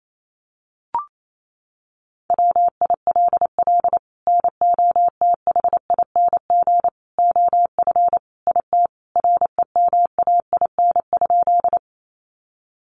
Morse code, heard as a keyed tone, about 28 wpm, 700 Hz. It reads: WILL NOTHING OF IT REMAIN?
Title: Sentences from Top 1200 Words 28wpm